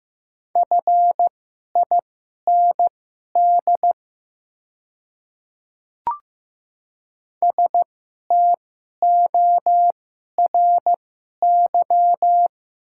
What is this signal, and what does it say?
Morse code, heard as a keyed tone, about 15 wpm, spaced as for 11 wpm, 700 Hz. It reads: FIND STORY
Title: 2x Character Spacing - Top 300 Words - 15wpm